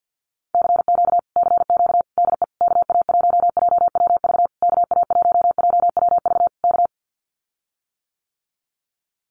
Morse code, heard as a keyed tone, about 50 wpm, 700 Hz. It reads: CQ CQ DE KA1JWV KA1JWV K
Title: Common QSO Phrases Encore 50wpm